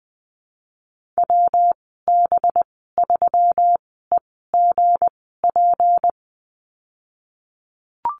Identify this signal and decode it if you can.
Morse code, heard as a keyed tone, about 20 wpm, spaced as for 15 wpm, 700 Hz. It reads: WB3EGP
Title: 231 Lesson 37g - Letter x - Call Signs 20wpm